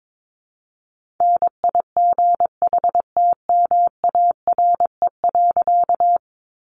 Morse code, heard as a keyed tone, about 22 wpm, 700 Hz. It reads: NIGHTMARE.
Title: A Sound Of Thunder 22wpm